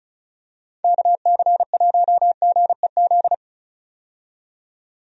Morse code, heard as a keyed tone, about 36 wpm, 700 Hz. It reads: KC1GEZ
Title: USA Call Signs - 6 Characters - 35wpm